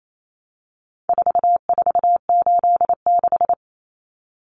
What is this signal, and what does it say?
Morse code, heard as a keyed tone, about 28 wpm, 700 Hz. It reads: 4486
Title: Numbers - 4-Digit 28wpm